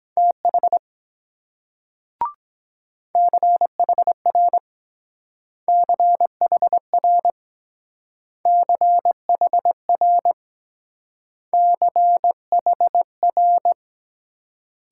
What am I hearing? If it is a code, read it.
Morse code, heard as a keyed tone, about 22 wpm, 700 Hz. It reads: TH CHR CHR CHR CHR